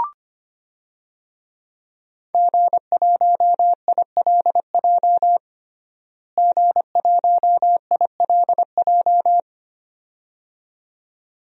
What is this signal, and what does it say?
Morse code, heard as a keyed tone, about 25 wpm, 700 Hz. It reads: G1ILJ G1ILJ
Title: DXCC-100 Call Signs - 5 Characters - Repeated 2x - 25wpm